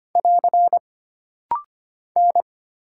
Morse code, heard as a keyed tone, about 25 wpm, 700 Hz. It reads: <AR> N
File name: Single Character (Letter:Number:Punctuation:Prosign) - Rapid-Fire - 25wpm